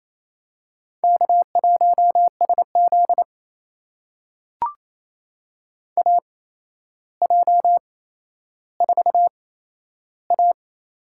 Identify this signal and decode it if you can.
Morse code, heard as a keyed tone, about 28 wpm, spaced as for 12 wpm, 700 Hz. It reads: K1SZ AJ4A
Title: USA Call Signs - 4 Characters - 8x Character Spacing - 28wpm